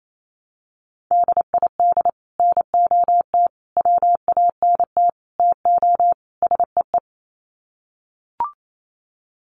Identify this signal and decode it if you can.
Morse code, heard as a keyed tone, about 28 wpm, 700 Hz. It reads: DID NOT WANT TO SEE
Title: Sets of 5 Words Encore2 28wpm